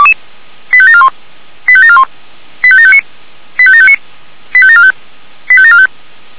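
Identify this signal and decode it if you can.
Five-tone selective call (ZVEI), heard as quick runs of five tones, tones of about 70 ms. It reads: 85741 85741 85759 85759 85745 85745
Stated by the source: ZVEI five-tone selective call received by SDR, from audio.f85360.feuerwehr-orlamuende.org.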